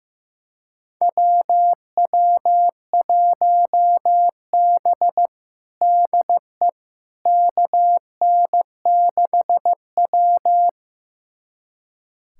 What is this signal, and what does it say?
Morse code, heard as a keyed tone, about 15 wpm, 700 Hz. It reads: WW1B DE KN6W
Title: USA Callsign DE Callsign - 4 Characters - 15wpm